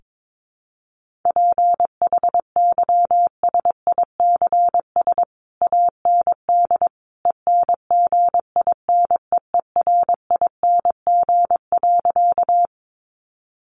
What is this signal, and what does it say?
Morse code, heard as a keyed tone, about 22 wpm, 700 Hz. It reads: PHYSICS AND ENGINEERING.
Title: Sentences from Top 1500 Words 22wpm